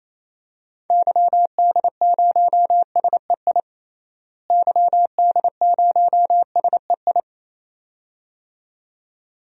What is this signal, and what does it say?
Morse code, heard as a keyed tone, about 28 wpm, 700 Hz. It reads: YD0SEI YD0SEI
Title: DXCC-100 Call Signs - 6 Characters - Repeated 2x - 28wpm